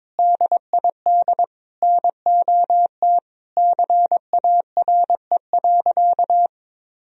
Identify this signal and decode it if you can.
Morse code, heard as a keyed tone, about 22 wpm, 700 Hz. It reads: DID NOT CARE.
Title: Sentences from Top 1000 Words - Review Part 3 - 22wpm